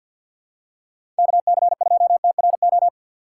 Morse code, heard as a keyed tone, about 45 wpm, 700 Hz. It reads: KC2TRO
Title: USA Call Signs - 6 Characters - 3x Character Spacing - 50wpm